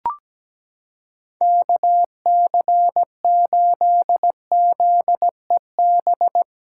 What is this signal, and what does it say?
Morse code, heard as a keyed tone, about 17 wpm, 700 Hz. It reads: KC8ZEB
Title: USA Callsign DE Callsign 17wpm